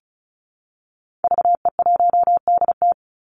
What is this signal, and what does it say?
Morse code, heard as a keyed tone, about 34 wpm, 700 Hz. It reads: VE1DT
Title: Call Signs Repeated 3x - 35wpm